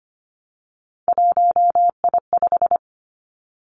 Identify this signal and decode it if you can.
Morse code, heard as a keyed tone, about 25 wpm, 700 Hz. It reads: 1I5
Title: Three Challenging Letter-Number Combinations 25wpm